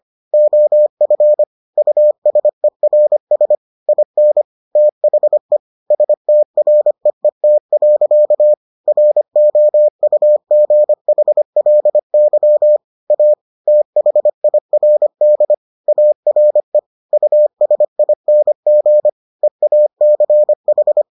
Morse code, heard as a keyed tone, about 25 wpm, 600 Hz. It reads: OF USERS IN THE STREET. ROUGHLY A THIRD ARE USING EACH